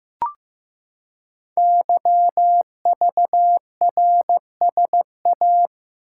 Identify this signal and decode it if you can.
Morse code, heard as a keyed tone, about 15 wpm, 700 Hz. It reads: YVRSA
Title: Code Groups - 5 Letters - 15wpm